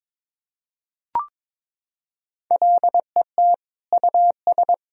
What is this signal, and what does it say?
Morse code, heard as a keyed tone, about 22 wpm, 700 Hz. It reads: LET US
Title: Sentences from Top 1000 Words - Review Part 4 - 22wpm